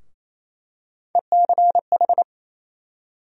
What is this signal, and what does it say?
Morse code, heard as a keyed tone, about 28 wpm, 700 Hz. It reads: ECH
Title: USA Call Sign Postfixes - 3 Letters 28wpm